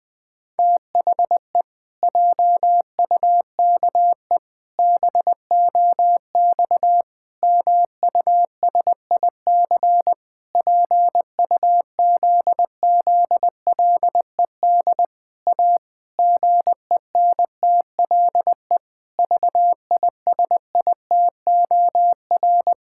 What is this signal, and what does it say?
Morse code, heard as a keyed tone, about 20 wpm, 700 Hz. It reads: THE JUKE BOX MUSIC PUZZLED A GENTLE VISITOR
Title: Pangram Challenge v2 - 20wpm